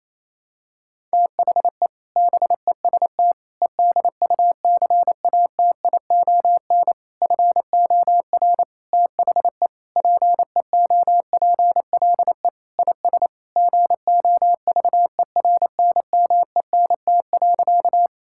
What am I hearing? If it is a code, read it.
Morse code, heard as a keyed tone, about 28 wpm, 700 Hz. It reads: THE BEST EDUCATION FOR THE PEOPLE IS GOVERNMENT.
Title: Sentences from Top 1000 Words - Review Part 6 - 28wpm